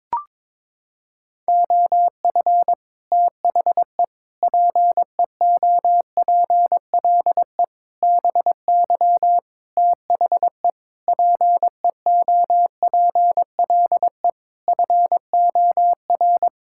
Morse code, heard as a keyed tone, about 23 wpm, 700 Hz. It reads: OF THE PEOPLE BY THE PEOPLE FOR
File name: Sets of 9 Words - 22wpm